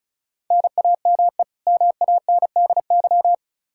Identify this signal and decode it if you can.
Morse code, heard as a keyed tone, about 36 wpm, 700 Hz. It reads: NAME MANDY
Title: Practice QSO - Exchange 1 - Beginning - 35wpm